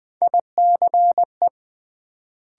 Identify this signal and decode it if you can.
Morse code, heard as a keyed tone, about 20 wpm, 700 Hz. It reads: ICE
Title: Build A Word - 1 Letter At A Time - 3 to 5-Letter Words - 20wpm